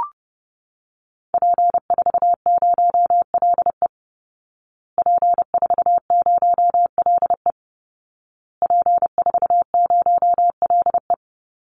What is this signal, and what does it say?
Morse code, heard as a keyed tone, about 28 wpm, 700 Hz. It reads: P40LE P40LE P40LE
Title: Call Signs Repeated 2x - 30wpm